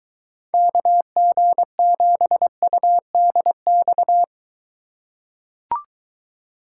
Morse code, heard as a keyed tone, about 22 wpm, 700 Hz. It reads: KG7UDX